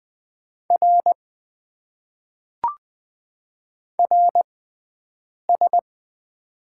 Morse code, heard as a keyed tone, about 20 wpm, spaced as for 7 wpm, 700 Hz. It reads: R RS
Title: Top 90 QSO Elements - 6x Character Spacing - 20wpm